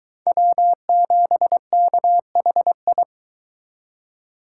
Morse code, heard as a keyed tone, about 23 wpm, 700 Hz. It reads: W7KHI